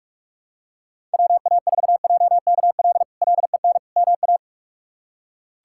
Morse code, heard as a keyed tone, about 47 wpm, 700 Hz. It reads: WA4JKL LEN MA
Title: K1USN SST QSO Practice - 45wpm